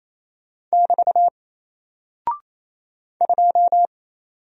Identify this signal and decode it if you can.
Morse code, heard as a keyed tone, about 28 wpm, 700 Hz. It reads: <BT> 2
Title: Single Character (Letter:Number:Punctuation:Prosign) Alt 28wpm